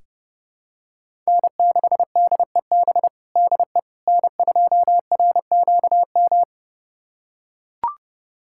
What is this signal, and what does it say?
Morse code, heard as a keyed tone, about 32 wpm, 700 Hz. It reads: N6DEB DE N2RQM
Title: USA Callsign DE Callsign - 5 Characters - 30wpm